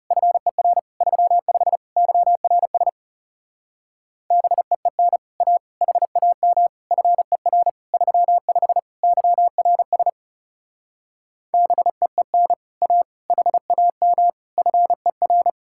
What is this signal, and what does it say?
Morse code, heard as a keyed tone, about 37 wpm, 700 Hz. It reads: FER 35 YRS BEEN A HAM FER 35 YRS BEEN A HAM FER
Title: Speed-Racing Common QSO Phrases Encore - 30wpm